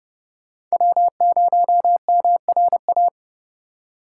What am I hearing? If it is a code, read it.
Morse code, heard as a keyed tone, about 39 wpm, 700 Hz. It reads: W0MRA